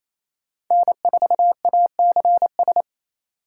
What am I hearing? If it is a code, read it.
Morse code, heard as a keyed tone, about 28 wpm, 700 Hz. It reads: N4ACS
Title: USA Call Signs - 5 Characters - 4x Character Spacing - 28wpm